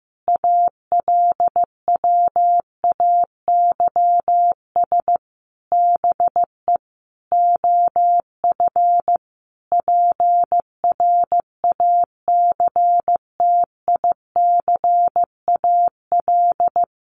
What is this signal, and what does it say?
Morse code, heard as a keyed tone, about 15 wpm, 700 Hz. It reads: ALWAYS BE OF PRACTICAL
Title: Sentences from Top 1400 Words 15wpm